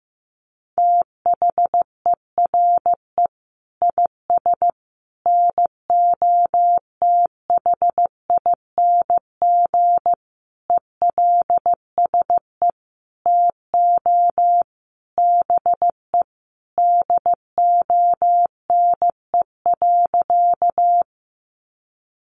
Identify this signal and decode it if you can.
Morse code, heard as a keyed tone, about 15 wpm, 700 Hz. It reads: THERE IS NOTHING ELSE TO BE DONE.
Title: Sentences from Top 800 Words 15wpm